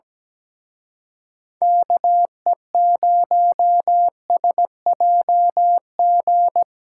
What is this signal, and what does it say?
Morse code, heard as a keyed tone, about 19 wpm, 700 Hz. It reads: KE0SJG
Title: Speed Racing - USA Call Signs 17wpm